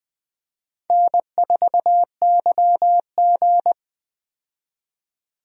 Morse code, heard as a keyed tone, about 21 wpm, 700 Hz. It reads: N4YG
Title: USA Call Signs - 4 Characters - 20wpm